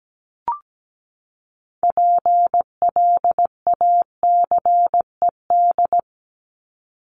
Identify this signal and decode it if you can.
Morse code, heard as a keyed tone, about 17 wpm, 700 Hz. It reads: PLACED
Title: ED Suffix - Top 100 Words with Base Words 17wpm